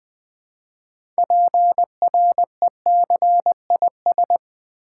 Morse code, heard as a keyed tone, about 20 wpm, 700 Hz. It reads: PRECIS
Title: Build A Word - 3 Letters At A Time - 9-Letter Words - 20wpm